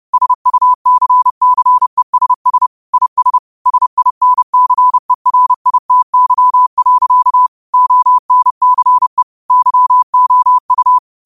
Morse code, heard as a keyed tone, about 30 wpm, 1000 Hz. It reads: SUCCESS IS SINCERITY. ONCE YOU
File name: qotdfiles / 30wpm